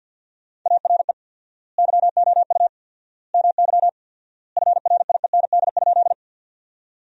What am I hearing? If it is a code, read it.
Morse code, heard as a keyed tone, about 51 wpm, 700 Hz. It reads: ARE YOU MY FRIEND?
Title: 4x Word-Spacing - Sentences from Top 300 Words - 50wpm